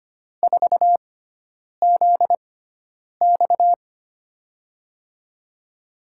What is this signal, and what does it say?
Morse code, heard as a keyed tone, about 25 wpm, spaced as for 9 wpm, 700 Hz. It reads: 4ZX
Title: USA Call Signs - 5 Characters - 6x Character Spacing - 25wpm